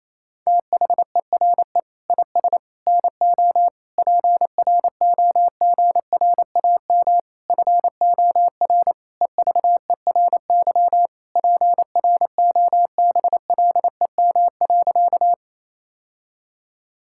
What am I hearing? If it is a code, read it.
Morse code, heard as a keyed tone, about 27 wpm, 700 Hz. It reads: THERE IS NO PROGRAM FOR EVERY PROBLEM.
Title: Sentences from Top 1000 Words - Review Part 5 - 28wpm